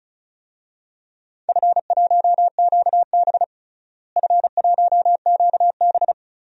Morse code, heard as a keyed tone, about 36 wpm, 700 Hz. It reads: F1QB F1QB
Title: DXCC-100 Call Signs - 4 Characters - Repeated 2x - 35wpm